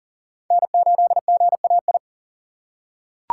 Morse code, heard as a keyed tone, about 40 wpm, 700 Hz. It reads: N8GAI
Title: USA Call Signs - 5 Characters - 4x Character Spacing - 40wpm